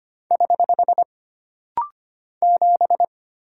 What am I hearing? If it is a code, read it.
Morse code, heard as a keyed tone, about 25 wpm, 700 Hz. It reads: <HH> 7
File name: Single Character (Letter:Number:Punctuation:Prosign) - Rapid-Fire - 25wpm